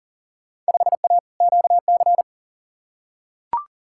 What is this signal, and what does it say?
Morse code, heard as a keyed tone, about 37 wpm, 700 Hz. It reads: 5A QC
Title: Field Day Exchanges Encore - 40wpm